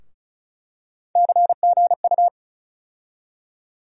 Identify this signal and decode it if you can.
Morse code, heard as a keyed tone, about 36 wpm, 700 Hz. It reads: CGU
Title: USA Call Sign Postfixes - 3 Letters 35wpm